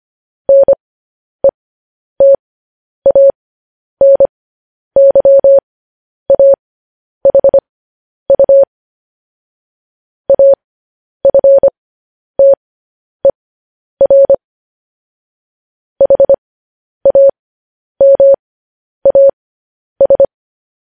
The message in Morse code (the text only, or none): NETANYAHU AFTER HAMAS